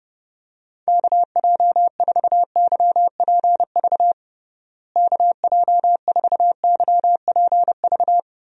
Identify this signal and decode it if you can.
Morse code, heard as a keyed tone, about 30 wpm, 700 Hz. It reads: KJ4YPV KJ4YPV
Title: USA Call Signs - 6 Characters - Repeated 3x - 30wpm